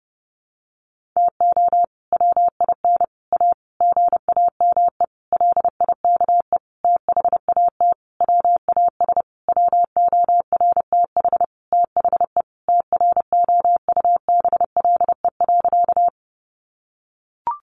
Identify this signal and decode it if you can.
Morse code, heard as a keyed tone, about 30 wpm, 700 Hz. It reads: TO WIN A GAME LIKE THAT WAS WORTH THE TROUBLE.
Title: Sentences from Top 1500 Words 30wpm